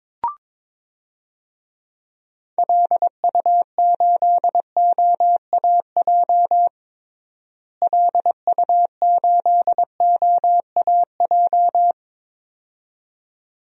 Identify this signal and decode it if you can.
Morse code, heard as a keyed tone, about 22 wpm, 700 Hz. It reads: LU8OAJ LU8OAJ
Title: DXCC-100 Call Signs - 6 Characters - Repeated 2x - 22wpm